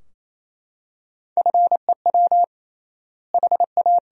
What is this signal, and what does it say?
Morse code, heard as a keyed tone, about 28 wpm, 700 Hz. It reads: FEW HA
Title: 3x Word-Spacing - Sentences from Top 300 Words - 28wpm